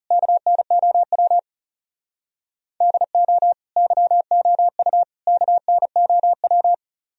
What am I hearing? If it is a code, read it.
Morse code, heard as a keyed tone, about 36 wpm, 700 Hz. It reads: KNOW DO YOU KNOW
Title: Speed-Racing - Sets of 3 Words 30wpm